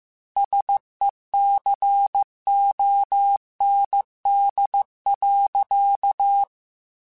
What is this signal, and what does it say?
Morse code, heard as a keyed tone, about 15 wpm, 800 Hz. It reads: SECOND.
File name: Technical Subjects / Technical Subjects-mp3s / Reactance @13